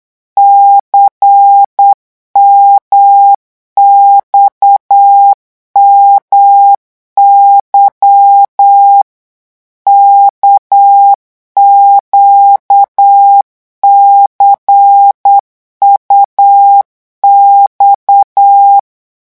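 Morse code, heard as a keyed tone, about 8 wpm, 800 Hz. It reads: CMXMY KQCUX
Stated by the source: Morse code revision lesson at 8 words per minute, from a French amateur radio course.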